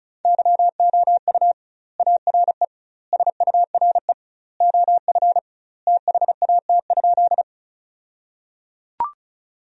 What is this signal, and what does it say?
Morse code, heard as a keyed tone, about 34 wpm, 700 Hz. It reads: YOU ARE SURE OF THAT?
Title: 2x Word-Spacing - Sentences from Top 300 Words - 35wpm